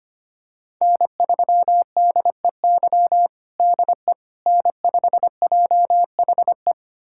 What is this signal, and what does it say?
Morse code, heard as a keyed tone, about 25 wpm, 700 Hz. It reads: N3DEY DE N5JHE
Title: USA Callsign DE Callsign - 5 Characters - 25wpm